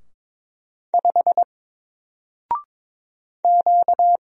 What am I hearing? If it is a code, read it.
Morse code, heard as a keyed tone, about 22 wpm, 700 Hz. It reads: 5 Q
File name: Single-Letter-Number-22wpm